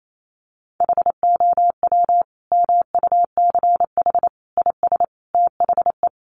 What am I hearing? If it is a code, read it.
Morse code, heard as a keyed tone, about 28 wpm, 700 Hz. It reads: HOW MUCH IS THE
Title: Sentences from Top 1200 Words 28wpm